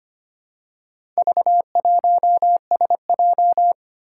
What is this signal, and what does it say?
Morse code, heard as a keyed tone, about 25 wpm, 700 Hz. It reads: V1SJ
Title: Four Challenging Letter-Number Combinations 25wpm